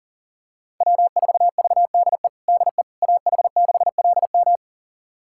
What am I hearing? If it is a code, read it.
Morse code, heard as a keyed tone, about 40 wpm, 700 Hz. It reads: W4VDE DE AH6LM
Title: USA Callsign DE Callsign - 5 Characters - 40wpm